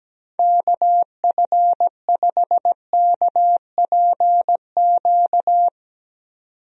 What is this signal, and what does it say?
Morse code, heard as a keyed tone, about 17 wpm, 700 Hz. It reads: KF5KPQ
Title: USA Call Signs - 6 Characters - Repeated 2x - 17wpm